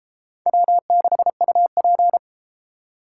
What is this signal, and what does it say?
Morse code, heard as a keyed tone, about 33 wpm, 700 Hz. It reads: W6UP